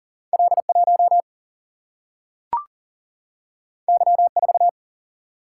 Morse code, heard as a keyed tone, about 39 wpm, 700 Hz. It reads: L1 Y4
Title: Single-Letter Single-Number 40wpm